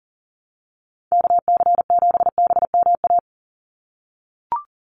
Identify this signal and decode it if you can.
Morse code, heard as a keyed tone, about 40 wpm, 700 Hz. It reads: KC7BMA
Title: USA Call Signs 40wpm